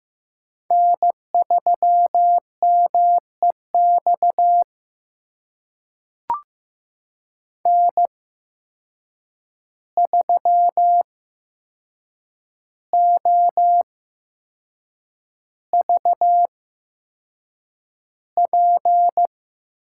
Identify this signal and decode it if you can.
Morse code, heard as a keyed tone, about 15 wpm, spaced as for 6 wpm, 700 Hz. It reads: N3MEX N3OVP
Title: USA Call Signs - 5 Characters - 8x Character Spacing - 15wpm